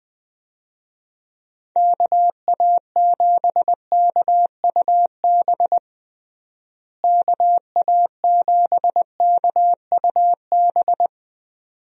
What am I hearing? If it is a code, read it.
Morse code, heard as a keyed tone, about 21 wpm, 700 Hz. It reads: KA7KUB KA7KUB
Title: USA Call Signs - 6 Characters - Repeated 2x - 20wpm